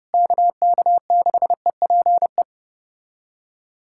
Morse code, heard as a keyed tone, about 29 wpm, 700 Hz. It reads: KK6EPE
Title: USA Callsign DE Callsign 30wpm